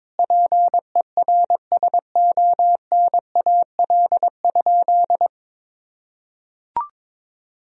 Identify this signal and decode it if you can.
Morse code, heard as a keyed tone, about 22 wpm, 700 Hz. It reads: PERSONAL?
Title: Sentences from Top 1000 Words - Review Part 4 - 22wpm